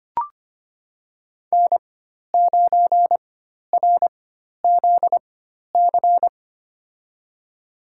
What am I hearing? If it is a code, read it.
Morse code, heard as a keyed tone, about 25 wpm, spaced as for 12 wpm, 700 Hz. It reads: N9RZC
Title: USA Call Signs - 5 Characters - 4x Character Spacing - 25wpm